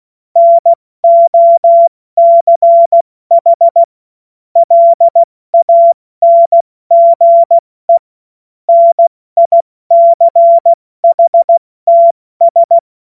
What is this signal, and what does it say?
Morse code, heard as a keyed tone, about 16 wpm, 680 Hz. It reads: NOCH LANGE NICHTS
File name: Morsekurs / Witze / Witz 02 April 2024 70 F80